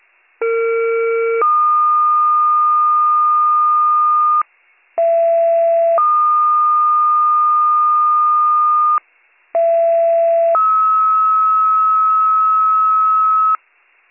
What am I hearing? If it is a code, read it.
Two-tone page, heard as a one-second tone, then a three-second tone, about 460 then 1220 Hz, about 670 then 1220 Hz, about 670 then 1290 Hz. It is heard from 0.4 to 13.6 s.